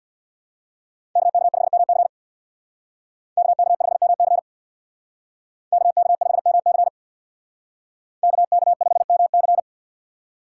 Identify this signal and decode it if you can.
Morse code, heard as a keyed tone, about 60 wpm, 700 Hz. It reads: KK5MC KK5MC KK5MC KK5MC
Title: Speed Racing - USA 2x2 Call Signs 50wpm